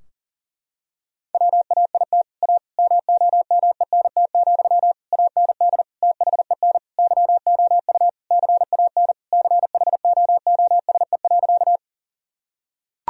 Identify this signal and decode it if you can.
Morse code, heard as a keyed tone, about 40 wpm, 700 Hz. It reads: WAIT A MOMENT, AND THEN YOU CAN CHOOSE.
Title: Sentences from Top 1000 Words - Review Part 4 - 40wpm